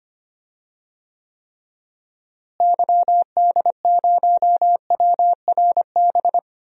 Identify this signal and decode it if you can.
Morse code, heard as a keyed tone, about 25 wpm, 700 Hz. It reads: YD0WRB
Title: DXCC-100 Call Signs - 6 Characters - Repeated 3x - 25wpm